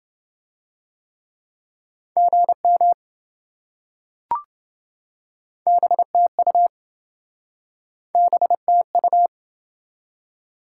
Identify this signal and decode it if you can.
Morse code, heard as a keyed tone, about 27 wpm, 700 Hz. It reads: GM BTU BTU